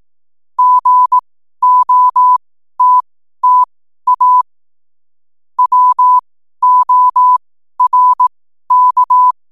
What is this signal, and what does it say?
Morse code, heard as a keyed tone, about 18 wpm, spaced as for 12 wpm, 1000 Hz. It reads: GOTTA WORK
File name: n7net / 12wpm